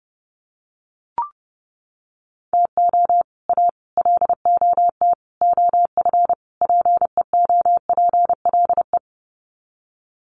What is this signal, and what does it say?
Morse code, heard as a keyed tone, about 30 wpm, 700 Hz. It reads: TO A LOT OF PEOPLE
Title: Sets of 5 Words Encore 30wpm